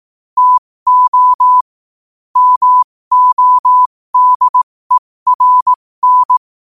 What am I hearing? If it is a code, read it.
Morse code, heard as a keyed tone, about 18 wpm, spaced as for 15 wpm, 1000 Hz. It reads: TO MODERN
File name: qotdfiles / 15wpm